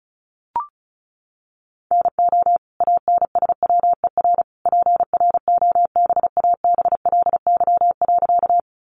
Morse code, heard as a keyed tone, about 34 wpm, 700 Hz. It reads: NO ANSWER PROBABLY.
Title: Sentences from Top 700 Words 35wpm